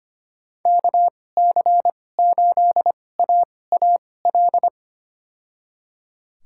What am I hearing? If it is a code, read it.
Morse code, heard as a keyed tone, about 25 wpm, spaced as for 18 wpm, 700 Hz. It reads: KC8AAL
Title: USA Call Signs - 6 Characters - 2x Character Spacing - 25wpm